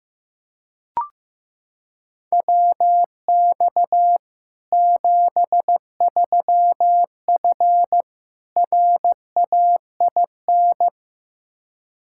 Morse code, heard as a keyed tone, about 15 wpm, 700 Hz. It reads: WX 73F RAIN
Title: Practice QSO - Exchange 2 - Info - 15wpm